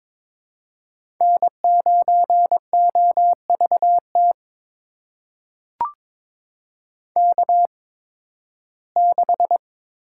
Morse code, heard as a keyed tone, about 22 wpm, spaced as for 14 wpm, 700 Hz. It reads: N9OVT K6
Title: USA Call Signs - 5 Characters - 8x Character Spacing - 22wpm